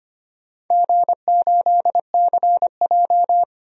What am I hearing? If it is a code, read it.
Morse code, heard as a keyed tone, about 25 wpm, 700 Hz. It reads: G8CJ
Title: DXCC-100 Call Signs - 4 Characters - 25wpm